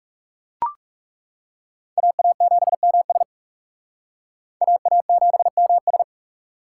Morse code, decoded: AA7MS AA7MS

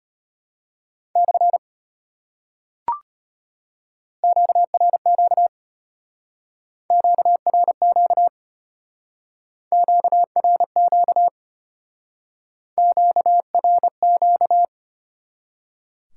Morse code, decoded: / QRQ QRQ QRQ QRQ

